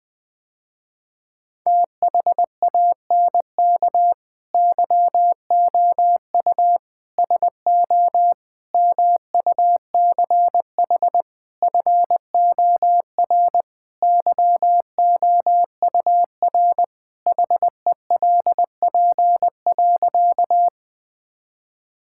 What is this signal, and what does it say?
Morse code, heard as a keyed tone, about 20 wpm, 700 Hz. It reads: THANK YOU SO MUCH FOR YOUR HELP.
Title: Sentences with 5-Word Phrases - Spoken Beginning - 20wpm